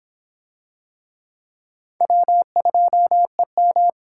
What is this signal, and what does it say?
Morse code, heard as a keyed tone, about 26 wpm, 700 Hz. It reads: W2EM